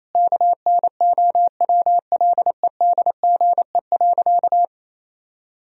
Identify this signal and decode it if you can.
Morse code, heard as a keyed tone, about 28 wpm, 700 Hz. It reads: KNOWLEDGE.
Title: Sentences from Top 1000 Words - Review Part 3 - 28wpm